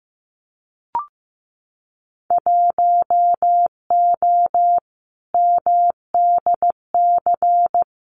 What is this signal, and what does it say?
Morse code, heard as a keyed tone, about 15 wpm, 700 Hz. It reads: 1O MDC
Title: Winter Field Day Exchanges 15wpm